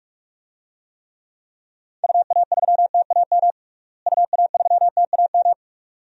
Morse code, heard as a keyed tone, about 43 wpm, 700 Hz. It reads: UA3TAM UA3TAM
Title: DXCC-100 Call Signs - 6 Characters - Repeated 2x - 45wpm